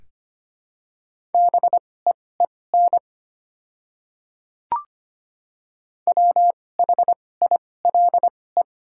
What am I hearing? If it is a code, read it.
Morse code, heard as a keyed tone, about 25 wpm, spaced as for 18 wpm, 690 Hz. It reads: BEEN WHILE
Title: 2x Character Spacing - Top 300 Words - 25wpm